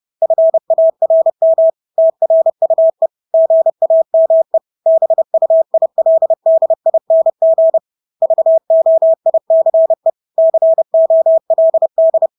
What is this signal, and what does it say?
Morse code, heard as a keyed tone, about 30 wpm, 650 Hz. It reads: FARM TRUE GAME BUILDING VOICE COLD
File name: Engelske ord 30-30 wpm